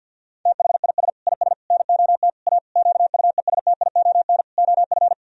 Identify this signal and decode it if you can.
Morse code, heard as a keyed tone, about 50 wpm, 700 Hz. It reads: THIS IS NOT A QUESTION OF
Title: Sentences from Top 1000 Words - Review Part 2 - 50wpm